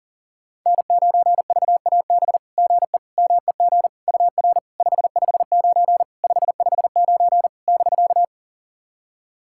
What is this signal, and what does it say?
Morse code, heard as a keyed tone, about 44 wpm, 700 Hz. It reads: N9VAB GE MEG UR 559 559 <BK>
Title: SOTA QSO Practice 40wpm